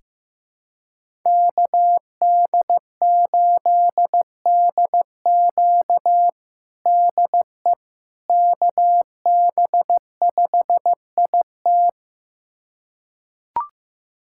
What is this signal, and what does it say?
Morse code, heard as a keyed tone, about 15 wpm, 700 Hz. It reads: KD8DQ DE KB5IT
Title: USA Callsign DE Callsign - 5 Characters - 15wpm